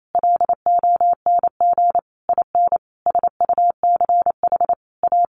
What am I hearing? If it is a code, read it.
Morse code, heard as a keyed tone, about 28 wpm, 700 Hz. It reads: LONG IN SUCH A
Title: Sentences from Top 1500 Words 28wpm